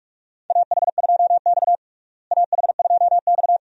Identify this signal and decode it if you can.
Morse code, heard as a keyed tone, about 46 wpm, 700 Hz. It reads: AH2X AH2X
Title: USA Call Signs - 4 Characters - Repeated 2x - 45wpm